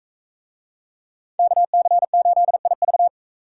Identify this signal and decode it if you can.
Morse code, heard as a keyed tone, about 42 wpm, 700 Hz. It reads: KC8IV